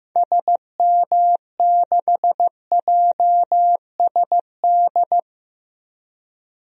Morse code, heard as a keyed tone, about 15 wpm, 700 Hz. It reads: SM6JSD
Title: DXCC-100 Call Signs - 6 Characters - 15wpm